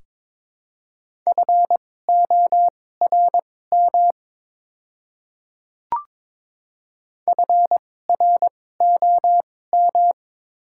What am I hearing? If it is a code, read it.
Morse code, heard as a keyed tone, about 22 wpm, spaced as for 16 wpm, 700 Hz. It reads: FORM FROM
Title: 2x Character Spacing - Top 200 Words - 22wpm